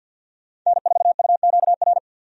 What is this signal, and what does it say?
Morse code, heard as a keyed tone, about 51 wpm, 700 Hz. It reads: N4UQR
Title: USA Call Signs - 5 Characters - 50wpm